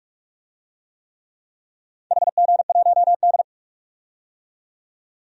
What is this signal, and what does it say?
Morse code, heard as a keyed tone, about 48 wpm, 700 Hz. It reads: HG1D
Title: DXCC-100 Call Signs - 4 Characters - 45wpm